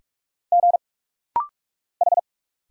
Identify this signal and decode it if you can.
Morse code, heard as a keyed tone, about 46 wpm, 700 Hz. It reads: G H
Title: Single Letters - Rapid-Fire 45wpm